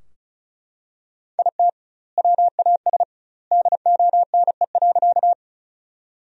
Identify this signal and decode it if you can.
Morse code, heard as a keyed tone, about 36 wpm, 700 Hz. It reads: IT WAS DONE.
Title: 2x Word-Spacing - Sentences from Top 500 Words - 35wpm